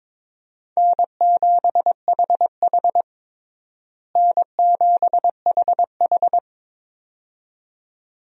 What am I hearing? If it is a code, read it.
Morse code, heard as a keyed tone, about 22 wpm, 700 Hz. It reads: N7HH N7HH
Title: USA Call Signs - 4 Characters - Repeated 2x - 22wpm